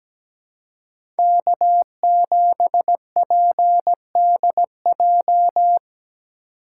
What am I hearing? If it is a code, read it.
Morse code, heard as a keyed tone, about 17 wpm, 700 Hz. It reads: K7PDJ
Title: USA Call Signs - 2x Character Spacing - 17wpm